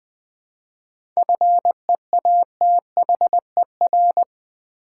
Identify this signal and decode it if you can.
Morse code, heard as a keyed tone, about 20 wpm, 700 Hz. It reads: FEATHER
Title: 7 Letter Words - First 3 Letters Given - 20wpm